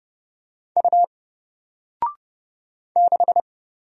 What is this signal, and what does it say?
Morse code, heard as a keyed tone, about 30 wpm, 700 Hz. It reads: U 6
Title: Single Letter-Number Alt 30wpm